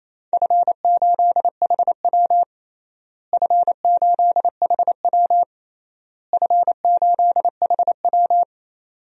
Morse code, heard as a keyed tone, about 28 wpm, 700 Hz. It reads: F8HW F8HW F8HW
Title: DXCC-100 Call Signs - 4 Characters - Repeated 3x - 28wpm